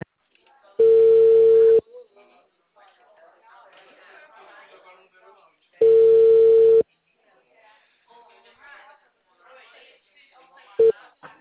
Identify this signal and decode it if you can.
Ringback tone (China, 450 Hz), with speech.
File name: out- -2007